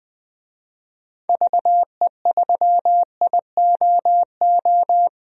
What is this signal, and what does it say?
Morse code, heard as a keyed tone, about 21 wpm, 700 Hz. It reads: VE3IOO
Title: DXCC-100 Call Signs - 6 Characters - 20wpm